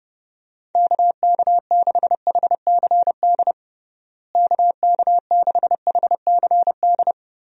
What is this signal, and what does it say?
Morse code, heard as a keyed tone, about 29 wpm, 700 Hz. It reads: KK6HCD KK6HCD
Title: USA Call Signs - 6 Characters - Repeated 3x - 30wpm